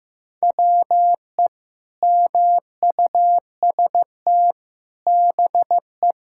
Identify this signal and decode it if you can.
Morse code, heard as a keyed tone, about 15 wpm, 700 Hz. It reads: WE MUST BE
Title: Sentences from Top 1600 Words 15wpm